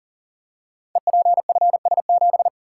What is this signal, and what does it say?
Morse code, heard as a keyed tone, about 43 wpm, 690 Hz. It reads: EPFS7
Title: Code Groups - 5 Letter:Numbers - 40wpm